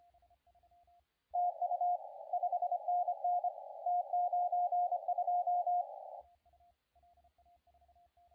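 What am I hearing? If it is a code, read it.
Morse code, heard as a keyed tone, about 25 wpm, 710 Hz, with radio noise. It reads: TU 5NN T92
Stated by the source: CW contest audio recorded at LA4C during the SAC CW contest.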